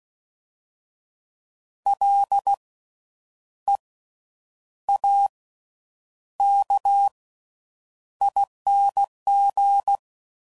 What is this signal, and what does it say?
Morse code, heard as a keyed tone, about 16 wpm, spaced as for 8 wpm, 800 Hz. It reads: LEAKING
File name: Suffixes / Special INGs / Slow With Faster Ings